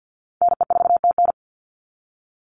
Morse code, heard as a keyed tone, about 49 wpm, 700 Hz. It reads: NE4TN
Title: NA SOTA Chaser Callsigns 50wpm